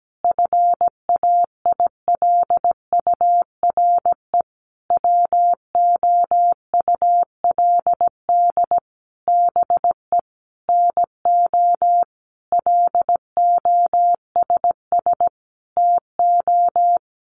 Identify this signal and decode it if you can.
Morse code, heard as a keyed tone, about 17 wpm, 700 Hz. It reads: FAILURE WOULD BE NO LOSS TO
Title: Sentences from Top 1400 Words 17wpm